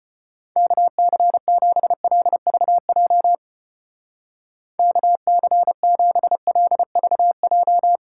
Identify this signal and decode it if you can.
Morse code, heard as a keyed tone, about 32 wpm, 700 Hz. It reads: KC7LVJ KC7LVJ